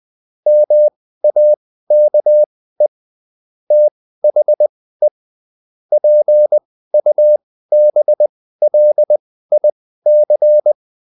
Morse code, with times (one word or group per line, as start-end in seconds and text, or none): 0.5-2.9 MAKE
3.7-5.1 THE
5.9-10.7 PUBLIC